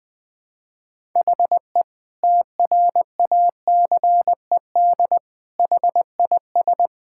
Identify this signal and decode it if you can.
Morse code, heard as a keyed tone, about 21 wpm, 700 Hz. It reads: HE TRACED HIS
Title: Sentences with 6-Word Phrases - 20wpm